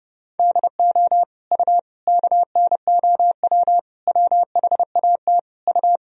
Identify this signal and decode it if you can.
Morse code, heard as a keyed tone, about 31 wpm, 700 Hz. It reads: DO U KNOW WHAT U
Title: Sentences from Top 100 Words and Abbreviations 30wpm